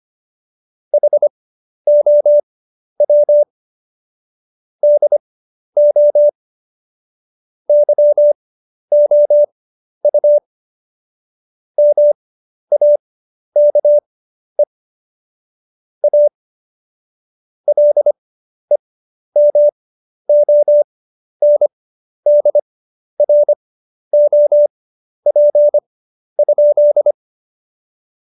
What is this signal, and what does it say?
Morse code, heard as a keyed tone, about 25 wpm, spaced as for 11 wpm, 590 Hz. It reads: HOW DO YOU MAKE A LEMONDROP?
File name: Jokes-25-char-10-eff